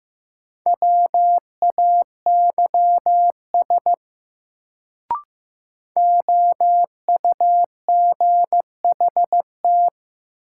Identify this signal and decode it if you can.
Morse code, heard as a keyed tone, about 15 wpm, 700 Hz. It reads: WAYS OUGHT
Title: Sets of 5 Words Encore - New Word Drill - 15wpm